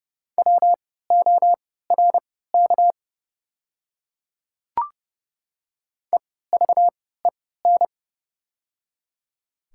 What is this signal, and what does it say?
Morse code, heard as a keyed tone, about 29 wpm, spaced as for 17 wpm, 690 Hz. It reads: WORK EVEN
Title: 3x Character Spacing - Top 200 Words - 30wpm